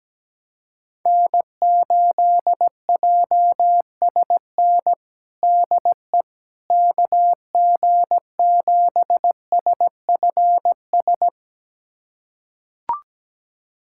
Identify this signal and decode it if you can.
Morse code, heard as a keyed tone, about 17 wpm, 700 Hz. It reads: N8JSN DE KG7SFS
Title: USA Callsign DE Callsign 17wpm